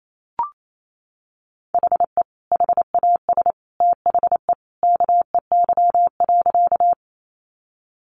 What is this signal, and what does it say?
Morse code, heard as a keyed tone, about 28 wpm, 700 Hz. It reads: HE HAS THE KEY.
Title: Sentences from Top 600 Words 28wpm